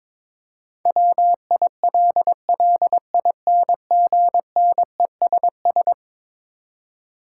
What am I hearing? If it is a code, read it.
Morse code, heard as a keyed tone, about 22 wpm, 700 Hz. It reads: WILLINGNESS
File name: practice / 22wpm / Eleven Letter Words - 22wpm